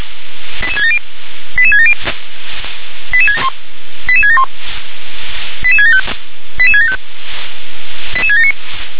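Five-tone selective call (ZVEI), heard as quick runs of five tones, tones of about 70 ms. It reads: 88670 88670 88671 88675 88675 88679
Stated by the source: ZVEI five-tone selective call 88776 received by SDR.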